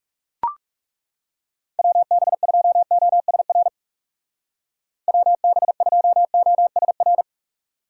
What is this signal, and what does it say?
Morse code, heard as a keyed tone, about 39 wpm, 700 Hz. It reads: WB2OSR WB2OSR